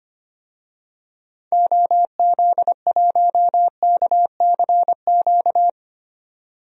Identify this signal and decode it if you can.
Morse code, heard as a keyed tone, about 25 wpm, 700 Hz. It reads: OZ1KCQ
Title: DXCC-100 Call Signs - 6 Characters - 25wpm